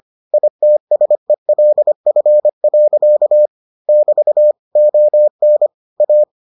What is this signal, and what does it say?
Morse code, heard as a keyed tone, about 25 wpm, 600 Hz. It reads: ITSELF. <BT> ON A